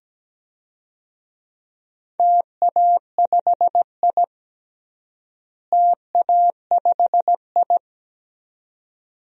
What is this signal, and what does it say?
Morse code, heard as a keyed tone, about 17 wpm, 700 Hz. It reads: TA5I TA5I
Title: DXCC-100 Call Signs - 4 Characters - Repeated 2x - 17wpm